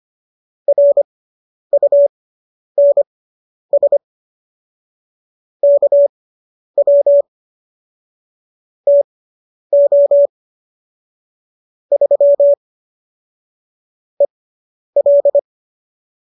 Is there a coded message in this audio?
Yes: Morse code from 0.7 to 15.4 s, heard as a keyed tone, about 25 wpm, spaced as for 10 wpm, 570 Hz. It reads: RUNS KW TO 3 EL